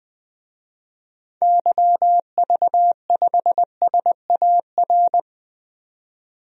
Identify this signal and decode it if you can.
Morse code, heard as a keyed tone, about 20 wpm, 700 Hz. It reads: YV5SAR
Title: DXCC-100 Call Signs - 6 Characters - Repeated 3x - 20wpm